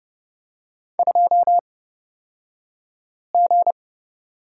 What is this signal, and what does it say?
Morse code, heard as a keyed tone, about 30 wpm, spaced as for 5 wpm, 700 Hz. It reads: 2G
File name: Lesson 37d - Letter x - Two Characters 30wpm